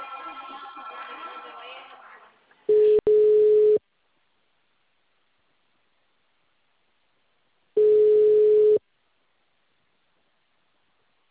Ringback tone (Europe, 425 Hz).
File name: out- -2044- - - .7628